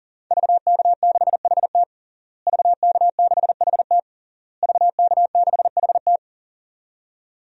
Morse code, heard as a keyed tone, about 40 wpm, 700 Hz. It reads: VK6HT VK6HT VK6HT
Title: DXCC-100 Call Signs - 5 Characters - Repeated 3x - 40wpm